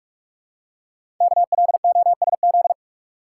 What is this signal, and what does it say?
Morse code, heard as a keyed tone, about 44 wpm, 700 Hz. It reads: KLOSZ
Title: Code Groups - 5 Letters - 45wpm